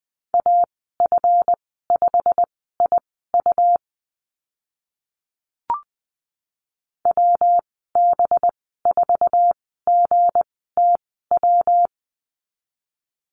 Morse code, heard as a keyed tone, about 20 wpm, spaced as for 14 wpm, 700 Hz. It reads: AF5IU WB4GTW
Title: 193 Lesson 31g - Letter v - Call Signs 20wpm